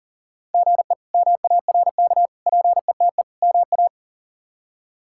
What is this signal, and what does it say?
Morse code, heard as a keyed tone, about 40 wpm, 690 Hz. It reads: GE MARK PETE MA
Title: K1USN SST QSO Practice - 40wpm